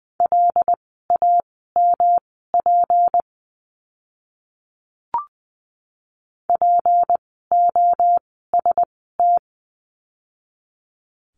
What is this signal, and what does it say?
Morse code, heard as a keyed tone, about 20 wpm, spaced as for 14 wpm, 700 Hz. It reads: LAMP POST
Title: 153 Lesson 25f - Letter p - Words 20wpm